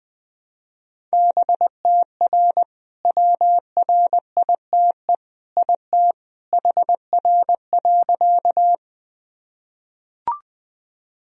Morse code, heard as a keyed tone, about 20 wpm, 700 Hz. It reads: BTR WRITE IT HR.
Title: Sentences from Top 400 Words and Abbreviations 20wpm